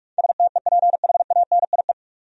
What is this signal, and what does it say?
Morse code, heard as a keyed tone, about 43 wpm, 700 Hz. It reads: STEPHANIE
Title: USA Common Names - 45wpm